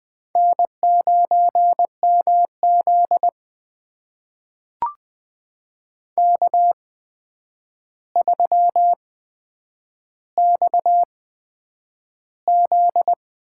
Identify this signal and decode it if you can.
Morse code, heard as a keyed tone, about 20 wpm, spaced as for 9 wpm, 700 Hz. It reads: N9MZ K3XZ
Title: USA Call Signs - 4 Characters - 8x Character Spacing - 20wpm